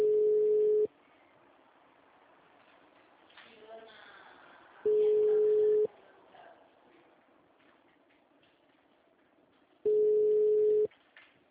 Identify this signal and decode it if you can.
Ringback tone (Europe, 425 Hz), with speech.
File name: out- -2015- - - .6